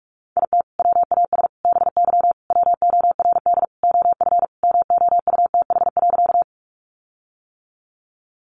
Morse code, heard as a keyed tone, about 44 wpm, 700 Hz. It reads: IT WAS BY WORD OF MOUTH.
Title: Sentences from Top 1500 Words 45wpm